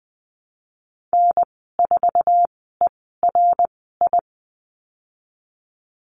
Morse code, heard as a keyed tone, about 20 wpm, spaced as for 15 wpm, 700 Hz. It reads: N4ERI
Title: 062 Lesson 11g - Letter h - Call Signs 20wpm